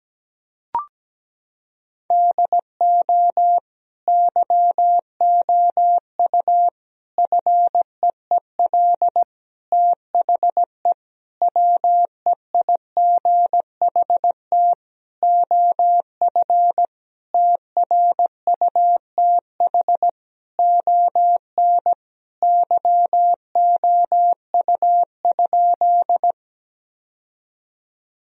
Morse code, heard as a keyed tone, about 17 wpm, 700 Hz. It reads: DO YOU FEEL THE WEIGHT OF TRUTH ON YOU?
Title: Sentences from Top 1000 Words - Review Part 3 - 17wpm